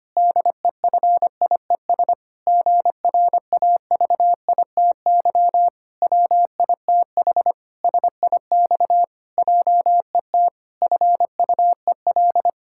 Morse code, heard as a keyed tone, about 25 wpm, 700 Hz. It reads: DEFIES GRAVITY WITH SIX JET FUEL
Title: Pangram Challenge v2 - 25wpm